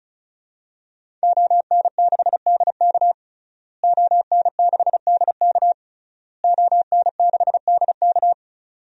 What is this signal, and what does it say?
Morse code, heard as a keyed tone, about 34 wpm, 700 Hz. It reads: ON6DK ON6DK ON6DK
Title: DXCC-100 Call Signs - 5 Characters - Repeated 3x - 35wpm